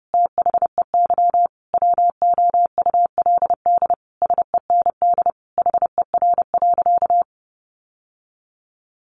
Morse code, heard as a keyed tone, about 30 wpm, 700 Hz. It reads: THEY WOULD SEND HER.
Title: Sentences from Top 1500 Words 30wpm